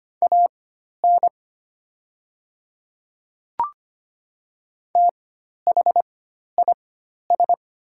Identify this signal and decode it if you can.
Morse code, heard as a keyed tone, about 25 wpm, spaced as for 12 wpm, 690 Hz. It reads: AN THIS
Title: 4x Character Spacing - Top 100 Words - 25wpm